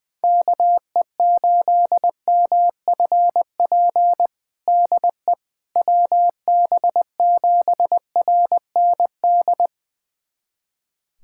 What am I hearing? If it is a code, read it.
Morse code, heard as a keyed tone, about 20 wpm, 700 Hz. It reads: KE8MFP DE WB7RND
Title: USA Callsign DE Callsign - 6 Characters - 20wpm